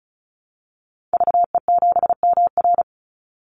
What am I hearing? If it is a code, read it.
Morse code, heard as a keyed tone, about 34 wpm, 700 Hz. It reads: VE7MR
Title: Call Signs Repeated 3x - 35wpm